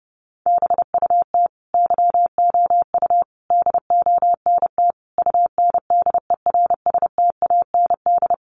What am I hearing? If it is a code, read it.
Morse code, heard as a keyed tone, about 30 wpm, 700 Hz. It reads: BUT YOU DONT UNDERSTAND
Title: Sentences from Top 600 Words 30wpm